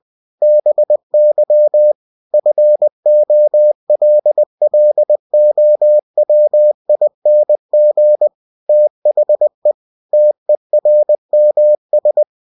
Morse code, heard as a keyed tone, about 20 wpm, 600 Hz. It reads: BY FOLLOWING THE TERMS
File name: cw-books / winnie-the-pooh / winnie-20wpm-page-0112